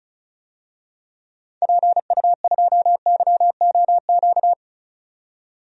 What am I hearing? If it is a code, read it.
Morse code, heard as a keyed tone, about 36 wpm, 700 Hz. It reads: PU2YOQ
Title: DXCC-100 Call Signs - 6 Characters - 35wpm